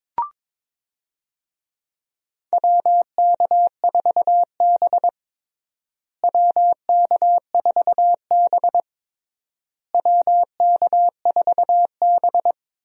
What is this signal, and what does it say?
Morse code, heard as a keyed tone, about 22 wpm, 700 Hz. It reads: WK4B WK4B WK4B
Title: USA Call Signs - 4 Characters - Repeated 3x - 22wpm